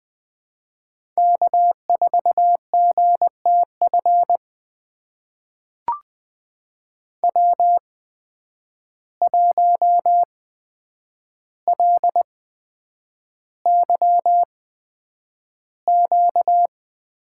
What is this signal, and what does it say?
Morse code, heard as a keyed tone, about 20 wpm, spaced as for 9 wpm, 700 Hz. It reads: K4GTF W1LYQ
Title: USA Call Signs - 5 Characters - 8x Character Spacing - 20wpm